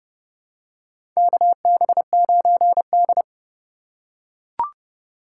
Morse code, heard as a keyed tone, about 30 wpm, 700 Hz. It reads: KB9D
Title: USA Call Signs - 4 Characters - 3x Character Spacing - 30wpm